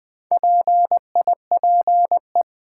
Morse code, heard as a keyed tone, about 20 wpm, 700 Hz. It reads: PIPE
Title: Build A Word - 4 Letters At A Time - 8-Letter Words - 20wpm